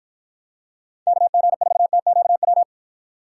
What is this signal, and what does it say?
Morse code, heard as a keyed tone, about 49 wpm, 700 Hz. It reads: KG4TQW